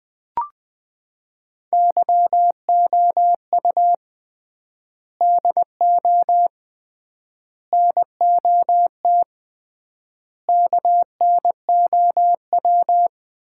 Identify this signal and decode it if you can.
Morse code, heard as a keyed tone, about 20 wpm, 700 Hz. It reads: YOU DO NOT KNOW
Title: 3x Word-Spacing - Sets of 4 Words - 20wpm